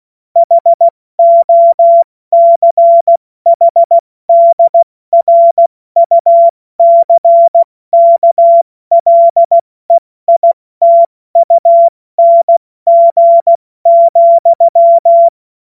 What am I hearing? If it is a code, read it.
Morse code, heard as a keyed tone, about 16 wpm, 680 Hz. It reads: HOCHDRUCKLEITUNG,
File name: Morsekurs / Witze / Witz 19 Dezember 2023 70 F80